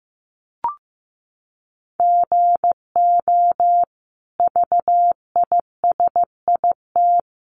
Morse code, heard as a keyed tone, about 15 wpm, 700 Hz. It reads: GO VISIT
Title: Sentences from Top 900 Words 15wpm